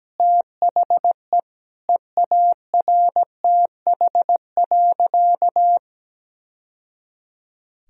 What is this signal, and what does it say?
Morse code, heard as a keyed tone, about 17 wpm, 700 Hz. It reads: THE EARTH.
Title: Sentences from Top 1000 Words - Review Part 3 - 17wpm